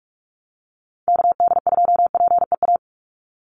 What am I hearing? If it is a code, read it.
Morse code, heard as a keyed tone, about 44 wpm, 700 Hz. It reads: KD2PEA
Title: USA Call Signs 45wpm